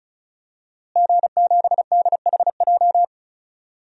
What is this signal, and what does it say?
Morse code, heard as a keyed tone, about 34 wpm, 700 Hz. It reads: G7DHJ
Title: DXCC-100 Call Signs - 5 Characters - Repeated 3x - 35wpm